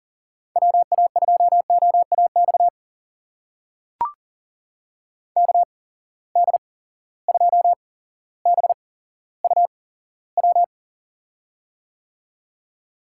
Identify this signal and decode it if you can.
Morse code, heard as a keyed tone, about 37 wpm, spaced as for 17 wpm, 700 Hz. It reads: WA2OAX KD2BUW
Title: USA Call Signs - 6 Characters - 8x Character Spacing - 40wpm